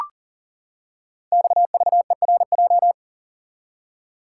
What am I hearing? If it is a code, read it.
Morse code, heard as a keyed tone, about 42 wpm, 700 Hz. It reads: XVERJ
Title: Code Groups - 5 Letters - 40wpm